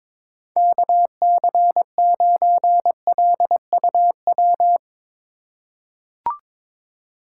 Morse code, heard as a keyed tone, about 22 wpm, 700 Hz. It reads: KC9LUW
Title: USA Call Signs - 6 Characters - 4x Character Spacing - 22wpm